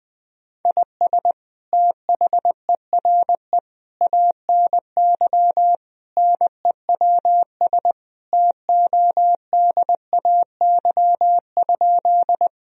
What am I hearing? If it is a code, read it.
Morse code, heard as a keyed tone, about 20 wpm, 700 Hz. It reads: IS THERE ANY NEWS TODAY?
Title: Sentences from Top 1000 Words - Review Part 4 - 20wpm